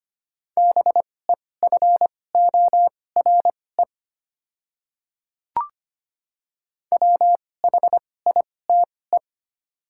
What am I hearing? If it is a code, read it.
Morse code, heard as a keyed tone, about 25 wpm, spaced as for 18 wpm, 700 Hz. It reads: BEFORE WHITE
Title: 2x Character Spacing - Top 300 Words - 25wpm